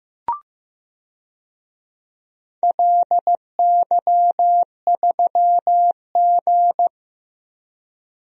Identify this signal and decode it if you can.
Morse code, heard as a keyed tone, about 15 wpm, 700 Hz. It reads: LY3G
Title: DXCC-100 Call Signs - 4 Characters - Repeated 2x - 15wpm